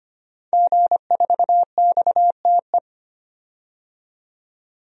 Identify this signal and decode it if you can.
Morse code, heard as a keyed tone, about 25 wpm, 700 Hz. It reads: G4XTE
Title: DXCC-100 Call Signs - 5 Characters - 25wpm